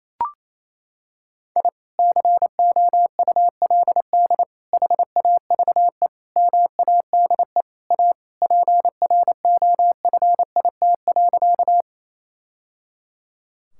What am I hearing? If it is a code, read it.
Morse code, heard as a keyed tone, about 28 wpm, 700 Hz. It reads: I COULD HAVE MADE A PROFIT.
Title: Sentences from Top 1000 Words - Review Part 5 - 28wpm